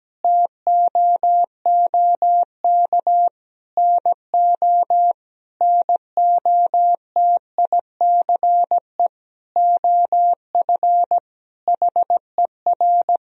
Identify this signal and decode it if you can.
Morse code, heard as a keyed tone, about 17 wpm, 700 Hz. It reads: TOOK NO NOTICE OF HER
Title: Sentences from Top 1000 Words - Review Part 2 - 17wpm